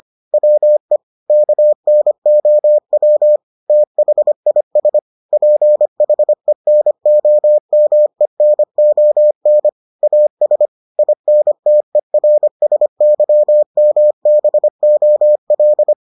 Morse code, heard as a keyed tone, about 25 wpm, 600 Hz. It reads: WE KNOW THIS PHENOMENON AS INTERSYMBOL